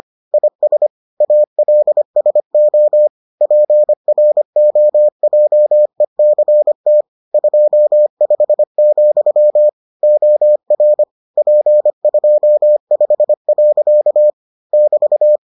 Morse code, heard as a keyed tone, about 25 wpm, 600 Hz. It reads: IS ALSO PROJECT 25, OR P25. <BT>